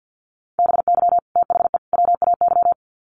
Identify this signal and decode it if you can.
Morse code, heard as a keyed tone, about 50 wpm, 700 Hz. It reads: BY THE WAY
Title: Sets of 3 Words 50wpm